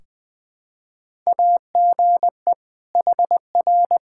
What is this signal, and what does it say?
Morse code, heard as a keyed tone, about 21 wpm, 700 Hz. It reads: AGE HR
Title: Practice QSO - Exchange 3 - More Info - 20wpm